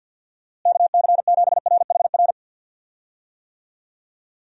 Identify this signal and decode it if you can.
Morse code, heard as a keyed tone, about 51 wpm, 700 Hz. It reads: KC7RHR
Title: USA Call Signs - 6 Characters - Repeated 3x - 50wpm